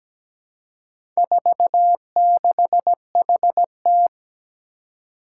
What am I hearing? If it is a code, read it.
Morse code, heard as a keyed tone, about 17 wpm, 700 Hz. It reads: 46HT
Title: Four Challenging Letter-Number Combinations 17wpm